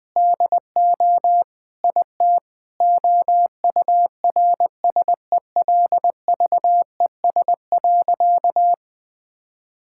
Morse code, heard as a keyed tone, about 20 wpm, 700 Hz. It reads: DO IT OURSELVES.
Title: Sentences from Top 1600 Words 20wpm